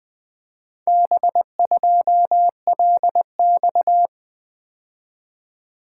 Morse code, heard as a keyed tone, about 20 wpm, 700 Hz. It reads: B2LX
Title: Four Letter-Number Combinations - 20wpm